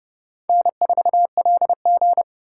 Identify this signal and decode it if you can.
Morse code, heard as a keyed tone, about 30 wpm, 700 Hz. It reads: N4LG
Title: USA Call Signs - 4 Characters - 4x Character Spacing - 30wpm